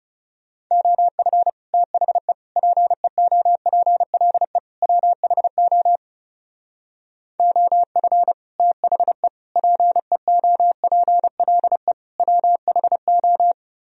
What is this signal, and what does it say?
Morse code, heard as a keyed tone, about 36 wpm, 700 Hz. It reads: OF THE PEOPLE WHO OF THE PEOPLE WHO
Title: Speed-Racing - Sets of 4 Words 30wpm